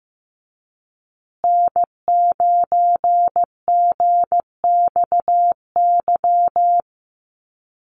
Morse code, heard as a keyed tone, about 15 wpm, 700 Hz. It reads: N9GXY
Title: USA Call Signs Repeated 3x 15wpm